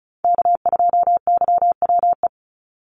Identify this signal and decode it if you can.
Morse code, heard as a keyed tone, about 36 wpm, 700 Hz. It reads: K2YWE
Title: Call Signs - 35wpm